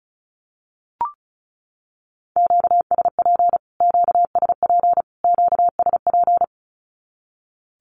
Morse code, heard as a keyed tone, about 34 wpm, 700 Hz. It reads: QSP QSP QSP
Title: Q-Codes Repeated 3x 35wpm